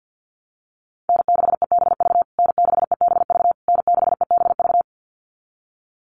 Morse code, heard as a keyed tone, about 48 wpm, 700 Hz. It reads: N6EBV N6EBV N6EBV
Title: USA Call Signs Repeated 3x 50wpm